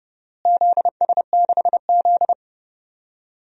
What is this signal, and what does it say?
Morse code, heard as a keyed tone, about 29 wpm, 700 Hz. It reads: ZS6Z
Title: DXCC-100 Call Signs - 4 Characters - Repeated 3x - 30wpm